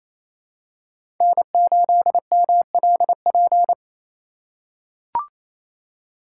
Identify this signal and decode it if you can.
Morse code, heard as a keyed tone, about 27 wpm, 700 Hz. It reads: N8MLP
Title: USA Call Signs - 5 Characters - 4x Character Spacing - 28wpm